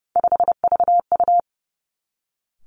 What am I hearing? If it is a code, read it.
Morse code, heard as a keyed tone, about 30 wpm, 700 Hz. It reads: 5VU
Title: Three Challenging Letter-Number Combinations 30wpm